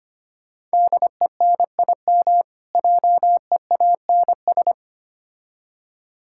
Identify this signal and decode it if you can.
Morse code, heard as a keyed tone, about 25 wpm, 700 Hz. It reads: DENIM JEANS
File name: practice / 25wpm / Difficult Words with Hints - 25wpm